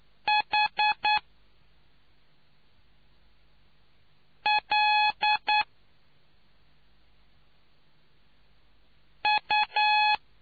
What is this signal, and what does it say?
Morse code, heard as a keyed tone, about 9 wpm, 840 Hz. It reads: H L U